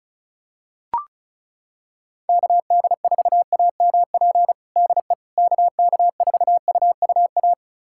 Morse code, heard as a keyed tone, about 36 wpm, 700 Hz. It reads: KD4AMP DE KK4UUA
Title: USA Callsign DE Callsign 35wpm